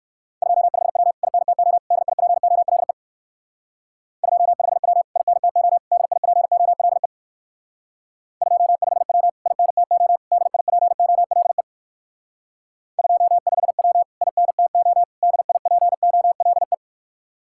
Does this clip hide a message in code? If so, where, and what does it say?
Yes: Morse code from 0.4 to 16.7 s, heard as a keyed tone, about 56 wpm, 700 Hz. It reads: 25W INTO DIPOLE 25W INTO DIPOLE 25W INTO DIPOLE 25W INTO DIPOLE